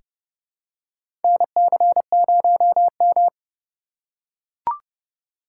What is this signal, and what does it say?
Morse code, heard as a keyed tone, about 30 wpm, 700 Hz. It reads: NC0M
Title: USA Call Signs - 4 Characters - 3x Character Spacing - 30wpm